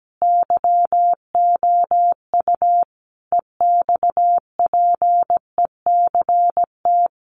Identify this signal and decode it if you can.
Morse code, heard as a keyed tone, about 17 wpm, 700 Hz. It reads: YOU EXPECT
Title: Sentences from Top 1100 Words 17wpm